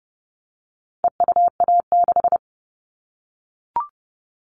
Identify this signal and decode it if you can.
Morse code, heard as a keyed tone, about 30 wpm, 700 Hz. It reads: EUA6
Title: All Cut Numbers - 4-Digit 30wpm